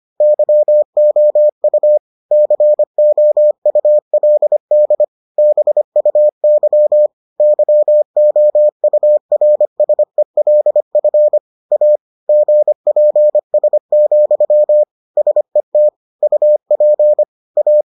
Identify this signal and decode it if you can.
Morse code, heard as a keyed tone, about 25 wpm, 600 Hz. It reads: YOU COULD BUY YOURSELF A GPS, SET UP A